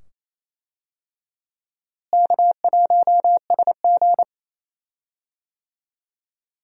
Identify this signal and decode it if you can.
Morse code, heard as a keyed tone, about 28 wpm, 700 Hz. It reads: K1SG
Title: USA Call Signs - 4 Characters - Repeated 2x - 28wpm